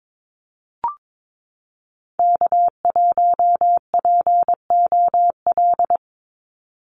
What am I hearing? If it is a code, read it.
Morse code, heard as a keyed tone, about 22 wpm, 700 Hz. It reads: K1POL
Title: USA Call Signs 22wpm